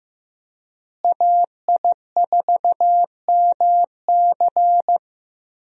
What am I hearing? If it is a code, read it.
Morse code, heard as a keyed tone, about 15 wpm, 700 Hz. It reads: AI4MC
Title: USA Call Signs - 5 Characters - Repeated 3x - 15wpm